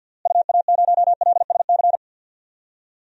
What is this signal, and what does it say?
Morse code, heard as a keyed tone, about 52 wpm, 700 Hz. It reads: UA0LSC
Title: DXCC-100 Call Signs - 6 Characters - Repeated 3x - 50wpm